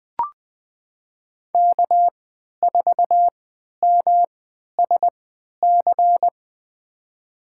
Morse code, heard as a keyed tone, about 20 wpm, spaced as for 11 wpm, 700 Hz. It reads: K4MSC
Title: USA Call Signs - 3x Character Spacing - 20wpm